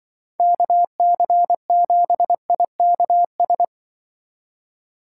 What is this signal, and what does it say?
Morse code, heard as a keyed tone, about 23 wpm, 700 Hz. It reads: KC7IKS
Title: Speed Racing - USA 2x3 Call Signs 20wpm